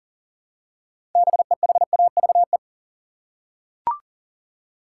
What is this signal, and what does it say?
Morse code, heard as a keyed tone, about 40 wpm, 700 Hz. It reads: BEHAVE
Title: Six Letter Words - 40wpm